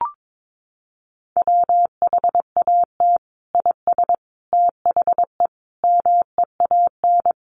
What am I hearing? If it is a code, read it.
Morse code, heard as a keyed tone, about 22 wpm, 700 Hz. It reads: WHAT IS THE MEAN
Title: Sentences from Top 900 Words 22wpm